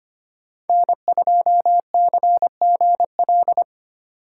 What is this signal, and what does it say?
Morse code, heard as a keyed tone, about 30 wpm, 700 Hz. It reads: N2CGL
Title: Speed Racing - USA 1x3 Call Signs 25wpm